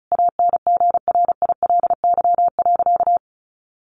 Morse code, heard as a keyed tone, about 34 wpm, 700 Hz. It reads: ANGRILY.
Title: A Sound Of Thunder 35wpm